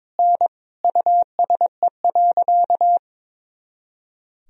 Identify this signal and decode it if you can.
Morse code, heard as a keyed tone, about 22 wpm, 700 Hz. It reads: N USE.
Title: Sentences from Top 100 Words and Abbreviations 22wpm